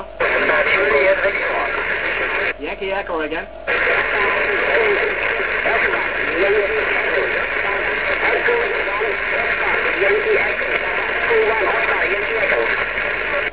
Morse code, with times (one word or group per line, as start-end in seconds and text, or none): none